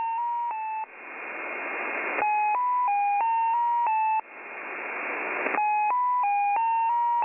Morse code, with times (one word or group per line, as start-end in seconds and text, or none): none